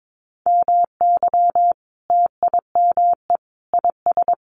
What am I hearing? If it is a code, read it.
Morse code, heard as a keyed tone, about 22 wpm, 700 Hz. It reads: MY TIME IS
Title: Sentences from Top 1100 Words 22wpm